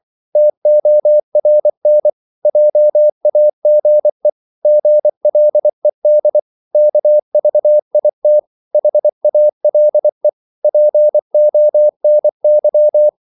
Morse code, heard as a keyed tone, about 24 wpm, 600 Hz. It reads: TORN JAGE GLED KVIT HALE PONY
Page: Korte danske ord 24wpm